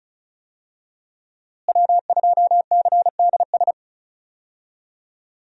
Morse code, heard as a keyed tone, about 36 wpm, 700 Hz. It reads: W2CDS
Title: USA Call Signs - 5 Characters - Repeated 3x - 35wpm